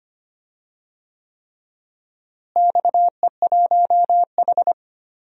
Morse code, heard as a keyed tone, about 25 wpm, 700 Hz. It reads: XE1H
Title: DXCC-100 Call Signs - 4 Characters - Repeated 3x - 25wpm